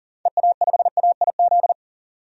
Morse code, heard as a keyed tone, about 40 wpm, 700 Hz. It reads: EA5AIZ
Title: DXCC-100 Call Signs - 6 Characters - 40wpm